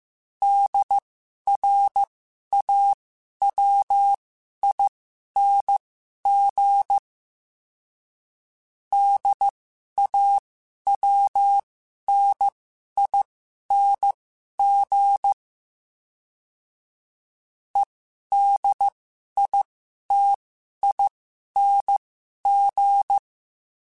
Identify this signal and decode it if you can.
Morse code, heard as a keyed tone, about 15 wpm, spaced as for 11 wpm, 800 Hz. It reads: DRAWING DAWNING EDITING
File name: Special INGs-mp3s / Special INGs-mp3s / Sngl Ing @10